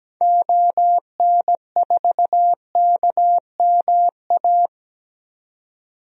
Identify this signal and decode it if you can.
Morse code, heard as a keyed tone, about 17 wpm, 700 Hz. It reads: ON4KMA
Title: DXCC-100 Call Signs - 6 Characters - 17wpm